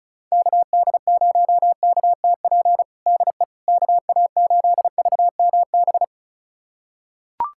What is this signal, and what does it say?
Morse code, heard as a keyed tone, about 34 wpm, 700 Hz. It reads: KD0KTP DE KA8VMB
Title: USA Callsign DE Callsign 35wpm